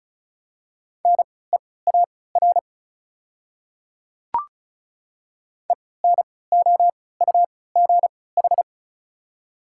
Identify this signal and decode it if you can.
Morse code, heard as a keyed tone, about 34 wpm, spaced as for 20 wpm, 700 Hz. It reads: NEAR ENOUGH
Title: 3x Character Spacing - Top 500 Words - 35wpm